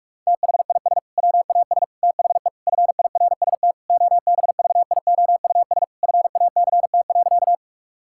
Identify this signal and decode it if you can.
Morse code, heard as a keyed tone, about 46 wpm, 700 Hz. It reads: THIS WAS THE FIRST OBVIOUS FACT.
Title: Sentences from Top 1600 Words 45wpm